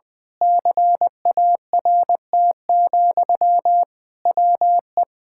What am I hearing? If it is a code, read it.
Morse code, heard as a keyed tone, about 20 wpm, 700 Hz. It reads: CART, WE
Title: Pangram Challenge v2 - 20wpm